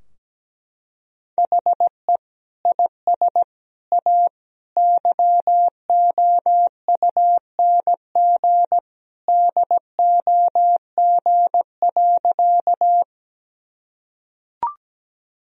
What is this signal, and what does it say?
Morse code, heard as a keyed tone, about 17 wpm, 700 Hz. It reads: HE IS A YOUNG DOG.
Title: Sentences from Top 1000 Words - Review Part 2 - 17wpm